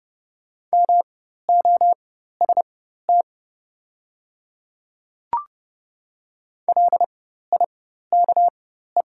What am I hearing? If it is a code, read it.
Morse code, heard as a keyed tone, about 31 wpm, spaced as for 14 wpm, 690 Hz. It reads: MOST LIKE
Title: 4x Character Spacing - Top 100 Words - 30wpm